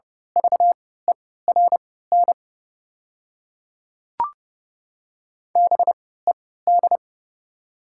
Morse code, heard as a keyed tone, about 32 wpm, spaced as for 17 wpm, 700 Hz. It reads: VERN BED
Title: 3x Character Spacing - Top 500 Words - 30wpm